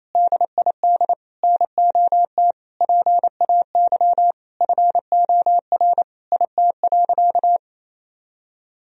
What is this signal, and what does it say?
Morse code, heard as a keyed tone, about 28 wpm, 700 Hz. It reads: DID NOT PAY FOR IT.
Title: Sentences from Top 1000 Words - Review Part 3 - 28wpm